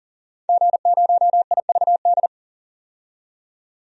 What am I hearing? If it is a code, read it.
Morse code, heard as a keyed tone, about 39 wpm, 700 Hz. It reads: G0IVD
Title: DXCC-100 Call Signs - 5 Characters - 40wpm